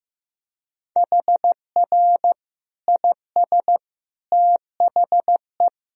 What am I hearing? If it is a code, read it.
Morse code, heard as a keyed tone, about 16 wpm, 700 Hz. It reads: HR IS THE
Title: Sentences from Top 300 Words and Abbreviations 15wpm